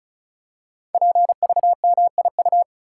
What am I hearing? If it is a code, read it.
Morse code, heard as a keyed tone, about 36 wpm, 700 Hz. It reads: PVMIU
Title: Code Groups - 5 Letter:Numbers - 35wpm